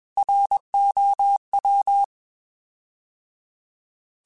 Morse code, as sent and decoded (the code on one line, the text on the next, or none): .-. --- .--
ROW